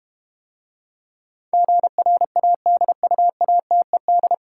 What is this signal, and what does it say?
Morse code, heard as a keyed tone, about 32 wpm, 700 Hz. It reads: GRADUATED